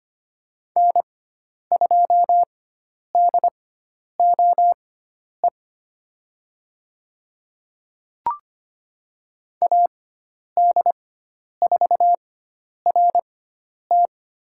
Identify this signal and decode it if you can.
Morse code, heard as a keyed tone, about 25 wpm, spaced as for 10 wpm, 700 Hz. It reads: N2DOE AD4RT
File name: course / 25:10wpm / 095 Lesson 16g - Letter c - Call Signs 25wpm